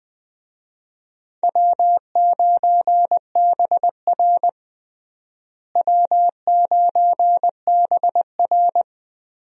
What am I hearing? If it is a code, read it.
Morse code, heard as a keyed tone, about 20 wpm, 700 Hz. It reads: W9BR W9BR
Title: USA Call Signs - 4 Characters - Repeated 3x - 20wpm